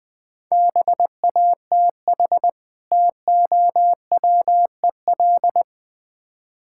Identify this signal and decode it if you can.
Morse code, heard as a keyed tone, about 22 wpm, 700 Hz. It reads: BATH TOWEL
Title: Difficult Words - Speed-Racing with Hints - 20wpm